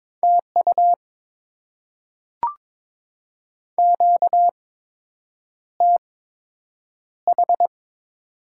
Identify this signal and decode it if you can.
Morse code, heard as a keyed tone, about 21 wpm, spaced as for 8 wpm, 700 Hz. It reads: TU QTH
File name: Top 20 QSO Elements - 8x Character Spacing - 22wpm